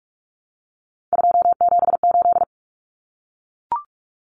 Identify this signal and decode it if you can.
Morse code, heard as a keyed tone, about 44 wpm, 700 Hz. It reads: 278
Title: Common Cut Numbers - 3-Digit 45wpm